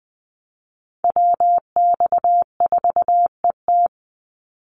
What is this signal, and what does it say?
Morse code, heard as a keyed tone, about 20 wpm, 690 Hz. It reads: WX4ET
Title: NA SOTA Chaser Callsigns 20wpm